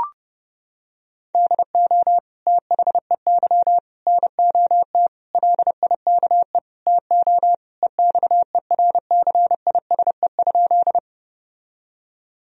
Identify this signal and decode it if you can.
Morse code, heard as a keyed tone, about 30 wpm, 700 Hz. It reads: DO THEY NOT LIKE TO EXERCISE?
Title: Sentences from Top 1000 Words - Review Part 4 - 30wpm